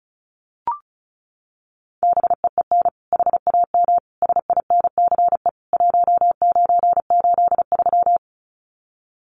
Morse code, heard as a keyed tone, about 34 wpm, 700 Hz. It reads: BEEN HAM SINCE 1983
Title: Common QSO Phrases 35wpm